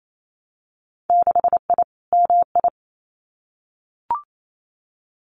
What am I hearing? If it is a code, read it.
Morse code, heard as a keyed tone, about 28 wpm, 700 Hz. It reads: 6I MI
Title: Winter Field Day Exchanges Encore 28wpm